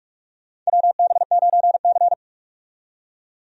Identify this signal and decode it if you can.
Morse code, heard as a keyed tone, about 46 wpm, 700 Hz. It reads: WB9C
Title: USA Call Signs - 4 Characters - 45wpm